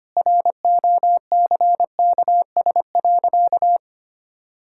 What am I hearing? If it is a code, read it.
Morse code, heard as a keyed tone, about 25 wpm, 700 Hz. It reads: ROCKS.
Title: Pangram Challenge v2 - 25wpm